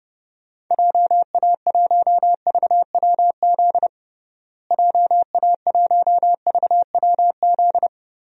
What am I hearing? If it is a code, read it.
Morse code, heard as a keyed tone, about 32 wpm, 700 Hz. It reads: JA1VWZ JA1VWZ
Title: DXCC-100 Call Signs - 6 Characters - Repeated 3x - 30wpm